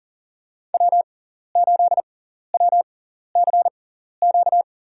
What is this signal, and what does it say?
Morse code, heard as a keyed tone, about 40 wpm, spaced as for 14 wpm, 700 Hz. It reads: W8WCQ
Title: USA Call Signs - 5 Characters - 6x Character Spacing - 40wpm